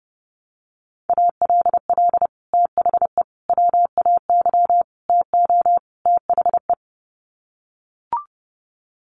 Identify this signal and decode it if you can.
Morse code, heard as a keyed tone, about 30 wpm, 700 Hz. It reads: ALL THE WAY TO THE
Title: Sets of 5 Words 30wpm